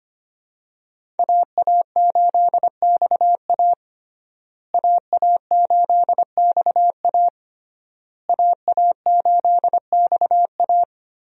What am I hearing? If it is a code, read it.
Morse code, heard as a keyed tone, about 25 wpm, 700 Hz. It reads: AA8XA AA8XA AA8XA
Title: USA Call Signs - 5 Characters - Repeated 3x - 25wpm